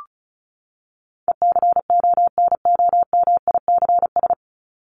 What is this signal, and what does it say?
Morse code, heard as a keyed tone, about 34 wpm, 700 Hz. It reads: ECONOMICS
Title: Top 501-750 Words 35wpm